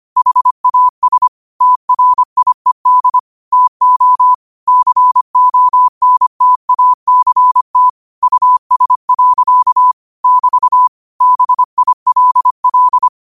Morse code, heard as a keyed tone, about 25 wpm, 1000 Hz. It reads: HAS TRIED TO CONTACT US. <BT> BILL
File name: qotdfiles / 25wpm